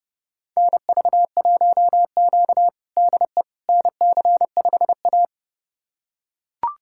Morse code, heard as a keyed tone, about 30 wpm, 700 Hz. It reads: NV1Q DE NC5A
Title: USA Callsign DE Callsign - 4 Characters - 30wpm